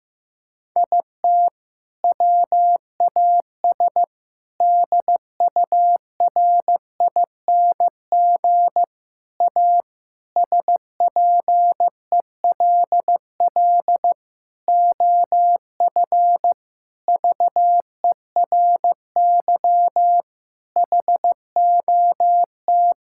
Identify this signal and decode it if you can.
Morse code, heard as a keyed tone, about 15 wpm, 700 Hz. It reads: IT WAS DURING A SPELL OF VERY HOT
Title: Sentences from Top 1000 Words - Review Part 5 - 15wpm